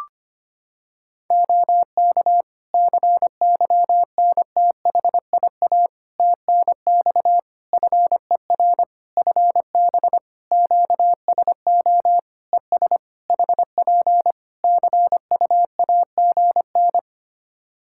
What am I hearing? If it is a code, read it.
Morse code, heard as a keyed tone, about 25 wpm, 700 Hz. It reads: OK CYNTHIA TNX FER FB QSO ES HP CUAGN
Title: K1USN SST QSO Practice 25wpm